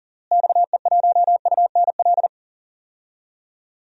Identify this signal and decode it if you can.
Morse code, heard as a keyed tone, about 40 wpm, 700 Hz. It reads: XE1UNL
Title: DXCC-100 Call Signs - 6 Characters - Repeated 3x - 40wpm